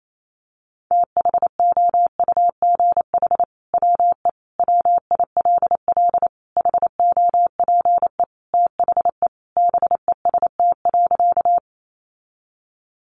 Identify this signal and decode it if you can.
Morse code, heard as a keyed tone, about 28 wpm, 700 Hz. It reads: THOUGH WE WILL HOPE THE BEST.
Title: Sentences from Top 1000 Words 28wpm